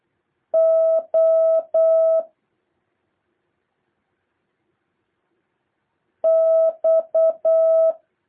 Morse code, heard as a keyed tone, about 8 wpm, 650 Hz. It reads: O X
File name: jh8cbh / 2023cw